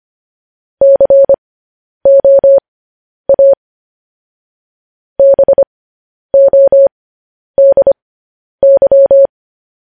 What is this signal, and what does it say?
Morse code, heard as a keyed tone, about 25 wpm, spaced as for 10 wpm, 550 Hz. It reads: COA BODY
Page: News Headlines in Morse Code at 10 WPM